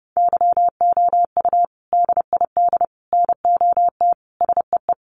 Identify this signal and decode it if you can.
Morse code, heard as a keyed tone, about 30 wpm, 700 Hz. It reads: YOU DID NOT SEE
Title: Sentences from Top 900 Words 30wpm